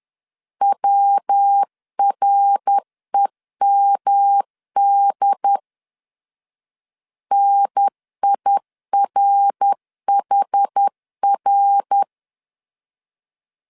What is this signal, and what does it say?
Morse code, heard as a keyed tone, about 10 wpm, 800 Hz. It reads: WREMD NIRHR
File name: CW / DET / 1 3 1500C S B 10wpm 3335